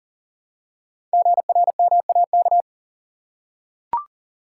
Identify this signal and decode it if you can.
Morse code, heard as a keyed tone, about 41 wpm, 700 Hz. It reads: GRMAK
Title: Code Groups - 5 Letters - 40wpm